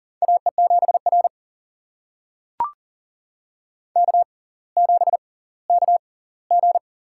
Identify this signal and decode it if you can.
Morse code, heard as a keyed tone, about 40 wpm, spaced as for 21 wpm, 700 Hz. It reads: AE7R K7KG
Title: USA Call Signs - 4 Characters - 6x Character Spacing - 40wpm